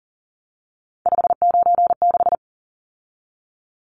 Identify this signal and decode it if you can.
Morse code, heard as a keyed tone, about 40 wpm, 700 Hz. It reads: 596
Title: Numbers - 3-Digit 40wpm